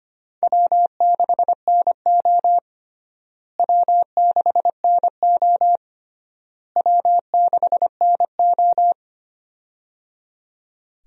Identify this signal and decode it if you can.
Morse code, heard as a keyed tone, about 25 wpm, 700 Hz. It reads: W6NO W6NO W6NO
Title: USA Call Signs - 4 Characters - Repeated 3x - 25wpm